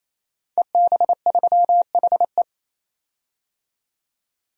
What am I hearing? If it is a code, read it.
Morse code, heard as a keyed tone, about 28 wpm, 700 Hz. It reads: EB3HE
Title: DXCC-100 Call Signs - 5 Characters - 28wpm